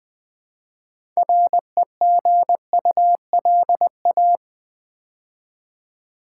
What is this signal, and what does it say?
Morse code, heard as a keyed tone, about 20 wpm, 700 Hz. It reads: REGULA
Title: Build A Word - 3 Letters At A Time - 9-Letter Words - 20wpm